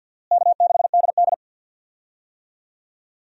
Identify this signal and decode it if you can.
Morse code, heard as a keyed tone, about 50 wpm, 700 Hz. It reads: K6DD
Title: USA Call Signs - 4 Characters - Repeated 2x - 50wpm